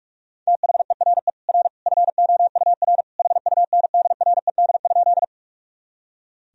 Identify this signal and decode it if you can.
Morse code, heard as a keyed tone, about 46 wpm, 700 Hz. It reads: THERE R FOUR HUNDRED?
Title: Sentences from Top 400 Words and Abbreviations 45wpm